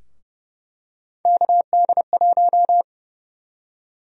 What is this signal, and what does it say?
Morse code, heard as a keyed tone, about 30 wpm, 700 Hz. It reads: KD1
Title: USA Call Sign Prefixes - 30wpm